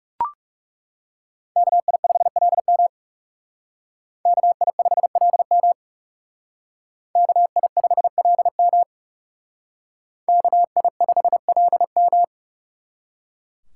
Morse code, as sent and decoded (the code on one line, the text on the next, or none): -.- .. ..... .-.. -- / -.- .. ..... .-.. -- / -.- .. ..... .-.. -- / -.- .. ..... .-.. --
KI5LM KI5LM KI5LM KI5LM